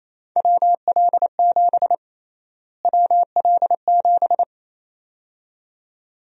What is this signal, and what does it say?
Morse code, heard as a keyed tone, about 28 wpm, 700 Hz. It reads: WL7 WL7
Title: USA Call Sign Prefixes - Repeated 2x - 28wpm